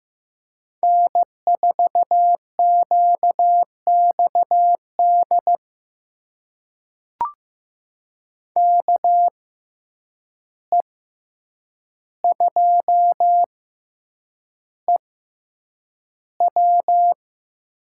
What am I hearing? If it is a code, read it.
Morse code, heard as a keyed tone, about 15 wpm, spaced as for 8 wpm, 700 Hz. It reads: N4QXD KE2EW
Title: USA Call Signs - 5 Characters - 6x Character Spacing - 15wpm